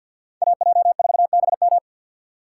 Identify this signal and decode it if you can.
Morse code, heard as a keyed tone, about 52 wpm, 700 Hz. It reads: AJ4BM
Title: USA Call Signs - 5 Characters - 3x Character Spacing - 50wpm